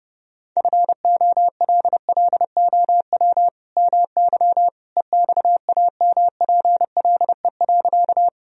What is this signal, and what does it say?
Morse code, heard as a keyed tone, about 30 wpm, 700 Hz. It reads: FOLLOW MY EXAMPLE.
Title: Sentences from Top 1000 Words - Review Part 1 - 30wpm